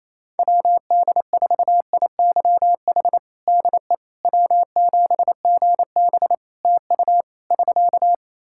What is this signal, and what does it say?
Morse code, heard as a keyed tone, about 28 wpm, 700 Hz. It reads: WD4IYH DE W7GB TU <SK>
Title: Practice QSO - Exchange 4 - Ending - 28wpm